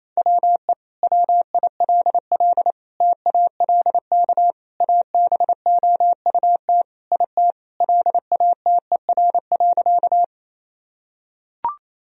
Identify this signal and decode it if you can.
Morse code, heard as a keyed tone, about 28 wpm, 700 Hz. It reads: WE WILL TALK ABOUT IT LATER.
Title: Sentences from Top 1000 Words - Review Part 4 - 28wpm